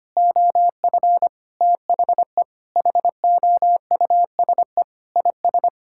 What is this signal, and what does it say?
Morse code, heard as a keyed tone, about 25 wpm, 700 Hz. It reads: OF THE HOUSE IS
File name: practice / 25wpm / Sentences with 3-Word Phrases - 25wpm